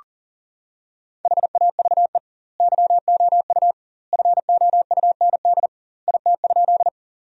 Morse code, heard as a keyed tone, about 40 wpm, 700 Hz. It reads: HAVE YOU FOUND IT?
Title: 2x Word-Spacing - Sentences from Top 300 Words - 40wpm